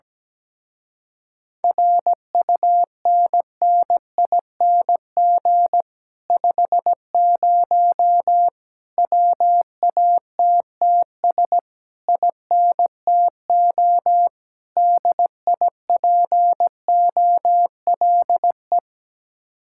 Morse code, heard as a keyed tone, about 17 wpm, 700 Hz. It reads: RUNNING 50 WATTS INTO DIPOLE
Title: Speed-Racing Common QSO Phrases - 17wpm